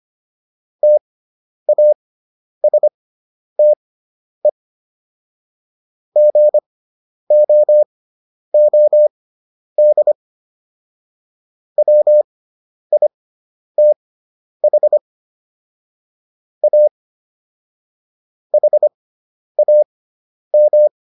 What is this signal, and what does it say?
Morse code, heard as a keyed tone, about 25 wpm, spaced as for 10 wpm, 600 Hz. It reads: TASTE GOOD WITH A HAM